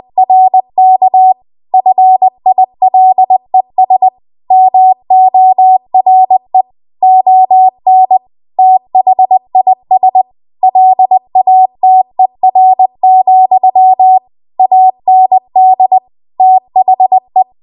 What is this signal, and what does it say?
Morse code, heard as a keyed tone, about 20 wpm, 750 Hz. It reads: RK FILES MORE ON THIS LATER, AND THE
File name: Archive / 20 WPM / 20WPM